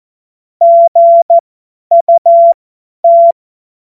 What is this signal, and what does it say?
Morse code, heard as a keyed tone, about 14 wpm, spaced as for 10 wpm, 680 Hz. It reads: GUT
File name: Morsekurs / Witze / Witz 13. Januar 2026 50 F70